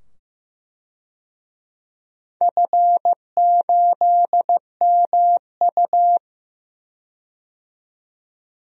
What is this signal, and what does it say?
Morse code, heard as a keyed tone, about 15 wpm, 700 Hz. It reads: F8MU
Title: DXCC-100 Call Signs - 4 Characters - 15wpm